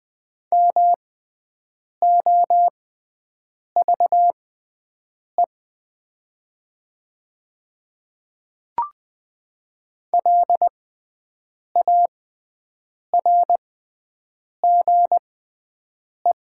Morse code, heard as a keyed tone, about 20 wpm, spaced as for 7 wpm, 700 Hz. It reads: MOVE LARGE
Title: 6x Character Spacing - Top 300 Words - 20wpm